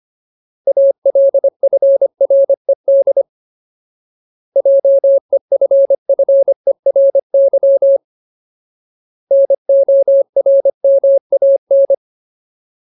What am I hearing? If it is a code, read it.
Morse code, heard as a keyed tone, about 25 wpm, 550 Hz. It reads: ALFRED JEFFERY NORMAN
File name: Ultimate Word List / 25WPM Ultimate